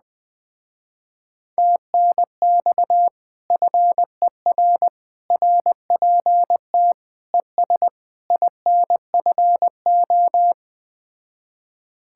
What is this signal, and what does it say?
Morse code, heard as a keyed tone, about 20 wpm, 700 Hz. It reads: TNX FER RPT ES INFO
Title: Speed-Racing Common QSO Phrases Encore - 17wpm